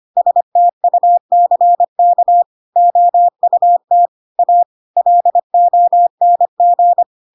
Morse code, heard as a keyed tone, about 25 wpm, 700 Hz. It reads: STUCK OUT A LONG
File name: Pinocchio / Pinocchio ch3 25 25 ik1hge